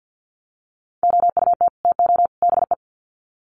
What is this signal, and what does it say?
Morse code, heard as a keyed tone, about 50 wpm, 700 Hz. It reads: OUT TO BE
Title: Sets of 3 Words 50wpm